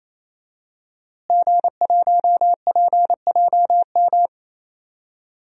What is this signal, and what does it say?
Morse code, heard as a keyed tone, about 28 wpm, 700 Hz. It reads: G1PJM
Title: DXCC-100 Call Signs - 5 Characters - Repeated 3x - 28wpm